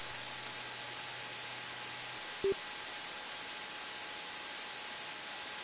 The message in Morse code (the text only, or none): none